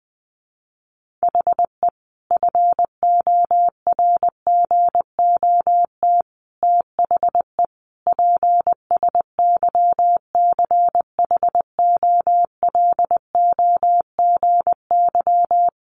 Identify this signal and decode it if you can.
Morse code, heard as a keyed tone, about 20 wpm, 700 Hz. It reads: HE FORGOT THE PSYCHOLOGY
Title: Sentences from Top 1400 Words 20wpm